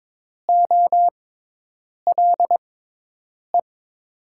Morse code, heard as a keyed tone, about 22 wpm, spaced as for 8 wpm, 700 Hz. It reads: OLE
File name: Top 90 QSO Elements - 6x Character Spacing - 22wpm